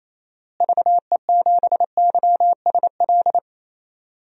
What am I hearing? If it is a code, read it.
Morse code, heard as a keyed tone, about 27 wpm, 700 Hz. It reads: VE7YSL
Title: DXCC-100 Call Signs - 6 Characters - Repeated 2x - 28wpm